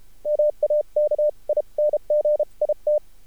Morse code, heard as a keyed tone, about 33 wpm, spaced as for 29 wpm, 600 Hz. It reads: MAKING IT
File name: archived / CRSnet